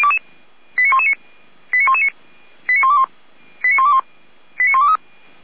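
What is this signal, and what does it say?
Five-tone selective call (ZVEI), heard as quick runs of five tones, tones of about 70 ms. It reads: 89119 89119 89121 89121 89123
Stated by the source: ZVEI five-tone selective call received by SDR, from the Feuerwehr Orlamünde audio server.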